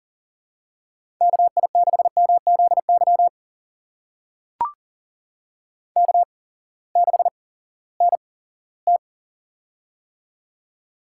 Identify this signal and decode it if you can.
Morse code, heard as a keyed tone, about 41 wpm, spaced as for 20 wpm, 700 Hz. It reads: KI6MZY K6NT
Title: USA Call Signs - 8x Character Spacing - 40wpm